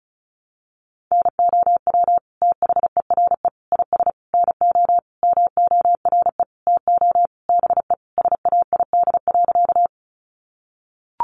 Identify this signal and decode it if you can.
Morse code, heard as a keyed tone, about 34 wpm, 700 Hz. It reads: NOW THERE IS NO MORE TO BE SAID.
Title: Sentences from Top 100 Words 35wpm